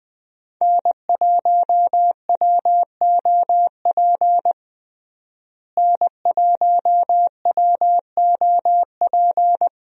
Morse code, heard as a keyed tone, about 20 wpm, 700 Hz. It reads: N1WOP N1WOP
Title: USA Call Signs - 5 Characters - Repeated 3x - 20wpm